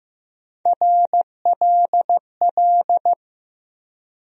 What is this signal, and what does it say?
Morse code, heard as a keyed tone, about 15 wpm, 700 Hz. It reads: RLL
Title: USA Call Sign Postfixes - 3 Letters 15wpm